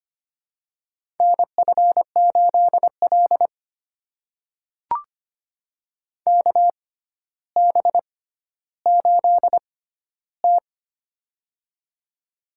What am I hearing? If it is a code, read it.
Morse code, heard as a keyed tone, about 25 wpm, spaced as for 13 wpm, 700 Hz. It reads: NF8L KB8T
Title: USA Call Signs - 4 Characters - 6x Character Spacing - 25wpm